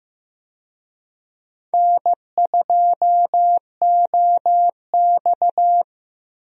Morse code, heard as a keyed tone, about 16 wpm, 700 Hz. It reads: N2OX
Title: USA Call Signs - 4 Characters - Repeated 3x - 15wpm